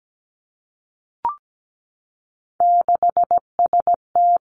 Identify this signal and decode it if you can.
Morse code, heard as a keyed tone, about 17 wpm, 700 Hz. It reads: 6ST
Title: Three Challenging Letter-Number Combinations 17wpm